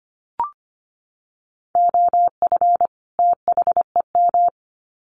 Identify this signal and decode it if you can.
Morse code, heard as a keyed tone, about 25 wpm, 700 Hz. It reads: OF THEM
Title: Sets of 2 Words 25wpm